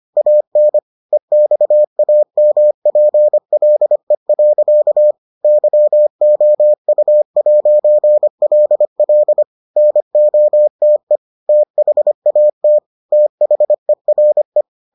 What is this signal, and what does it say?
Morse code, heard as a keyed tone, about 25 wpm, 600 Hz. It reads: AN EXAMPLE. YOU'LL NOTE THAT THERE